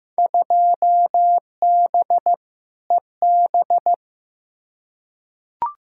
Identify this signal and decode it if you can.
Morse code, heard as a keyed tone, about 15 wpm, 700 Hz. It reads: 2B EB
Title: Field Day Exchanges - 15wpm